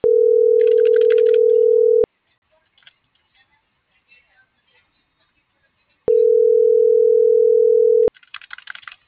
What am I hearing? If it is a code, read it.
Ringback tone (North America), 440 + 480 Hz.